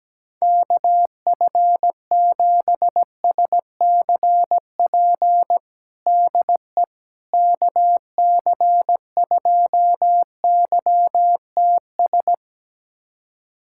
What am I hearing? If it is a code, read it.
Morse code, heard as a keyed tone, about 17 wpm, 700 Hz. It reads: KF7SCP DE KC2YTS
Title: USA Callsign DE Callsign - 6 Characters - 17wpm